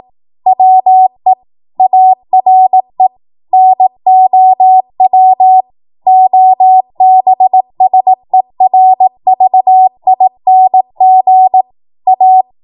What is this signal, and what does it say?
Morse code, heard as a keyed tone, about 18 wpm, 750 Hz. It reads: WE ARE NOW OBSERVING A
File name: Morse / 18WPM